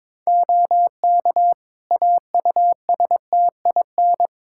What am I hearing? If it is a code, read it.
Morse code, heard as a keyed tone, about 22 wpm, 700 Hz. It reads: OK AUSTIN
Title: Practice QSO - Exchange 4 - Ending - 22wpm